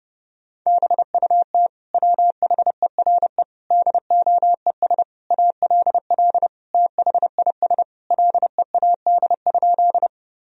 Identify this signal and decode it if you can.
Morse code, heard as a keyed tone, about 29 wpm, 700 Hz. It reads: BUT WHERE DOES ALL THIS LEAD?
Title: Sentences from Top 1000 Words - Review Part 2 - 30wpm